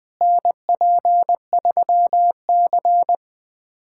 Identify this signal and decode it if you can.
Morse code, heard as a keyed tone, about 20 wpm, 700 Hz. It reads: NP3C
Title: USA Callsign DE Callsign - 4 Characters - 20wpm